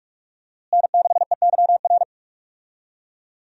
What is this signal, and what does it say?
Morse code, heard as a keyed tone, about 46 wpm, 700 Hz. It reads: N6EYR
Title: USA Call Signs - 5 Characters - Repeated 2x - 45wpm